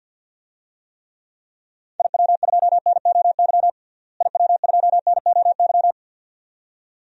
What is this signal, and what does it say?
Morse code, heard as a keyed tone, about 47 wpm, 700 Hz. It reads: IW2NOY IW2NOY
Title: DXCC-100 Call Signs - 6 Characters - Repeated 2x - 50wpm